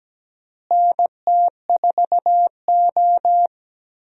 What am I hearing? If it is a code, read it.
Morse code, heard as a keyed tone, about 17 wpm, 700 Hz. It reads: NT4O
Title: USA Call Signs - 4 Characters - Repeated 2x - 17wpm